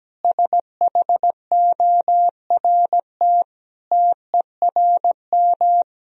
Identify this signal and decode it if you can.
Morse code, heard as a keyed tone, about 17 wpm, 700 Hz. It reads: SHORT TERM
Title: Sentences from Top 1000 Words - Review Part 4 - 17wpm